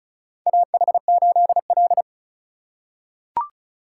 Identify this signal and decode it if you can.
Morse code, heard as a keyed tone, about 36 wpm, 700 Hz. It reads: AH8L
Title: USA Call Signs - 4 Characters - 4x Character Spacing - 35wpm